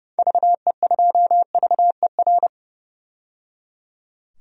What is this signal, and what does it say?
Morse code, heard as a keyed tone, about 30 wpm, 700 Hz. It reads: VE2VER
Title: DXCC-100 Call Signs - 6 Characters - 30wpm